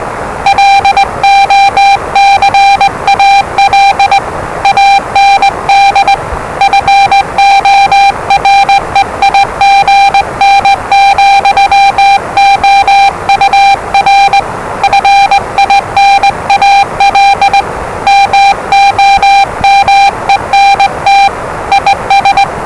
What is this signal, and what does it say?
Morse code, heard as a keyed tone, about 20 wpm, 790 Hz, with noise. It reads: LOCAL AND FOREIGN, OUR FINAL MOMENT IS
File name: WellingtonRadioZlwCloseDownOnHfAndMf / 01-Zlw 4277khz 417.5 qrt until 0925z